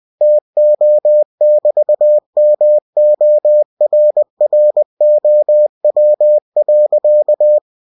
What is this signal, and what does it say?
Morse code, heard as a keyed tone, about 20 wpm, 600 Hz. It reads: TO<BT>MORROW.
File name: cw-books / winnie-the-pooh / winnie-20wpm-page-0105